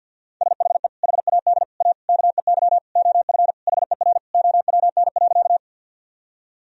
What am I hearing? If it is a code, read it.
Morse code, heard as a keyed tone, about 50 wpm, 700 Hz. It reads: SHE HAD A KEY OF HER OWN.
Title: Sentences from Top 1000 Words - Review Part 4 - 50wpm